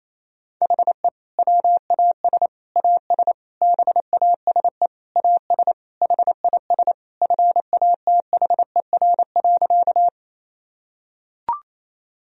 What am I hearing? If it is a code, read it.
Morse code, heard as a keyed tone, about 28 wpm, 700 Hz. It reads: HE WAS AS BASE AS HIS FATHER.
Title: Sentences from Top 1000 Words - Review Part 1 - 28wpm